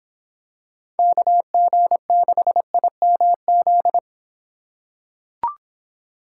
Morse code, heard as a keyed tone, about 26 wpm, 700 Hz. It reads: KG6IMZ